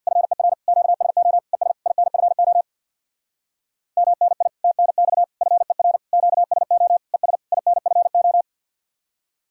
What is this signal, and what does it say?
Morse code, heard as a keyed tone, about 55 wpm, 700 Hz. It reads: FER QSO ES INFO MNI TNX FER QSO ES INFO
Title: Speed-Racing Common QSO Phrases Encore - 50wpm